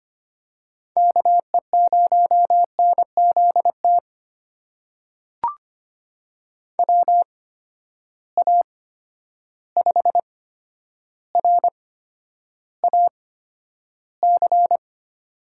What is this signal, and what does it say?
Morse code, heard as a keyed tone, about 25 wpm, spaced as for 11 wpm, 700 Hz. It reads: KE0NZT WA5RAC
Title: USA Call Signs - 6 Characters - 8x Character Spacing - 25wpm